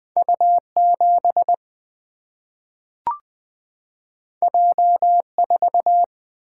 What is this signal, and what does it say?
Morse code, heard as a keyed tone, about 20 wpm, 700 Hz. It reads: U7 J4
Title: Single-Letter Single-Number 20wpm